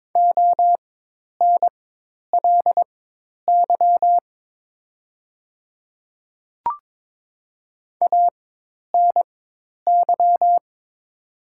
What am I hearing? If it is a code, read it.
Morse code, heard as a keyed tone, about 22 wpm, spaced as for 10 wpm, 700 Hz. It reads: ONLY ANY
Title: 4x Character Spacing - Top 200 Words - 22wpm